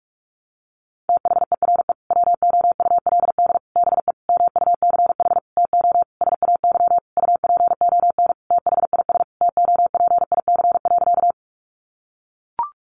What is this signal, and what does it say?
Morse code, heard as a keyed tone, about 44 wpm, 700 Hz. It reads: THERE WOULD BE MUCH TO SAY UPON THIS TOPIC.
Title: Sentences from Top 1200 Words 45wpm